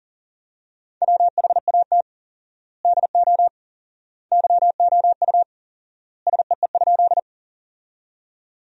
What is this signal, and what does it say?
Morse code, heard as a keyed tone, about 40 wpm, 700 Hz. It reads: WHAT DO YOU SEE?
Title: 4x Word-Spacing - Sentences from Top 100 Words - 40wpm